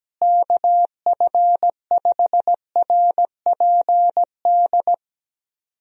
Morse code, heard as a keyed tone, about 17 wpm, 700 Hz. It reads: KF5RPD
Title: USA Call Signs - 6 Characters - Repeated 3x - 17wpm